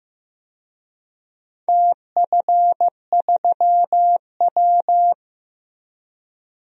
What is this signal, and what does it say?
Morse code, heard as a keyed tone, about 15 wpm, 700 Hz. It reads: TF3W
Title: DXCC-100 Call Signs - 4 Characters - Repeated 3x - 15wpm